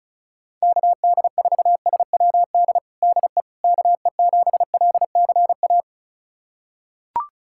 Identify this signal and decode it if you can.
Morse code, heard as a keyed tone, about 34 wpm, 700 Hz. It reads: KD4SWD DE KE7LCA
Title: USA Callsign DE Callsign - 6 Characters - 35wpm